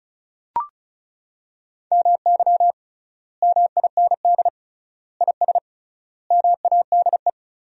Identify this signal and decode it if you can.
Morse code, heard as a keyed tone, about 34 wpm, 700 Hz. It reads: MY MIND IS MADE
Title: 3x Word-Spacing - Sentences from Top 500 Words - 35wpm